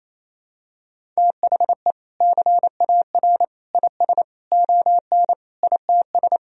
Morse code, heard as a keyed tone, about 28 wpm, 700 Hz. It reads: THE CAR IS ON ITS
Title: Sentences from Top 1000 Words - Review Part 1 - 28wpm